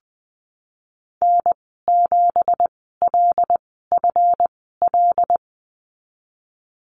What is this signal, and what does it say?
Morse code, heard as a keyed tone, about 20 wpm, spaced as for 14 wpm, 700 Hz. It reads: N7LFL
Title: 193 Lesson 31g - Letter v - Call Signs 20wpm